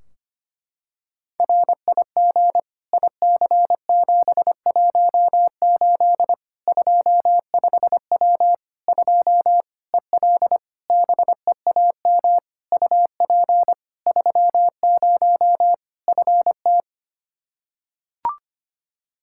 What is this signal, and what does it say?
Morse code, heard as a keyed tone, about 25 wpm, 700 Hz. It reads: RIG IC718 25W 2 EL BEAM UP 30 FT
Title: Practice QSO - Exchange 2 - Info - 25wpm